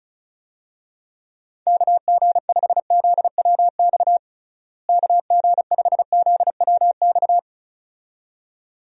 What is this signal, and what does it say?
Morse code, heard as a keyed tone, about 34 wpm, 700 Hz. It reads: KG5ZWX KG5ZWX
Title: USA Call Signs - 6 Characters - Repeated 2x - 35wpm